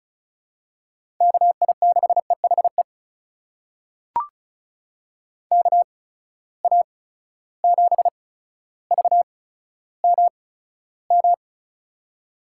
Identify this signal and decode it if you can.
Morse code, heard as a keyed tone, about 36 wpm, spaced as for 15 wpm, 700 Hz. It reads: KI6EHE KA7VMM
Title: USA Call Signs - 6 Characters - 8x Character Spacing - 35wpm